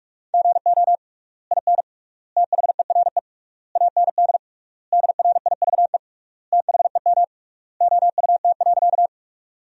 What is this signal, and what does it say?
Morse code, heard as a keyed tone, about 46 wpm, 700 Hz. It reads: GO IN THERE AND DRIVE THEM OUT.
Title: 3x Word-Spacing - Sentences from Top 500 Words - 45wpm